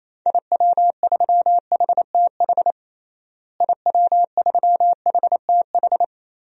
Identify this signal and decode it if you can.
Morse code, heard as a keyed tone, about 28 wpm, 700 Hz. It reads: IW3HTH IW3HTH
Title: DXCC-100 Call Signs - 6 Characters - Repeated 3x - 28wpm